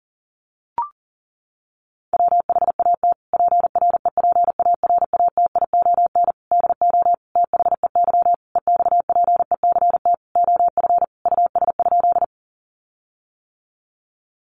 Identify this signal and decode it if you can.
Morse code, heard as a keyed tone, about 40 wpm, 700 Hz. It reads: WHAT PREPARATION DO THEY EXPECT OF US?
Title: Sentences from Top 1400 Words 40wpm